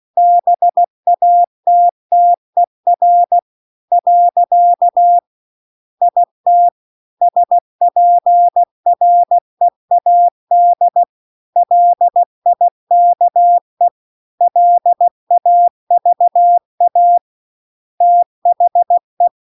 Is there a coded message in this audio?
Yes: Morse code from 0.2 to 19.3 s, heard as a keyed tone, about 16 wpm, 700 Hz. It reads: BATTER . IT SPREAD LIKE LAVA THE